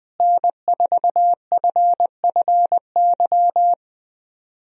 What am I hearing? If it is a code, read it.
Morse code, heard as a keyed tone, about 20 wpm, 700 Hz. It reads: N4FFY
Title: USA Call Signs - 5 Characters - Repeated 3x - 20wpm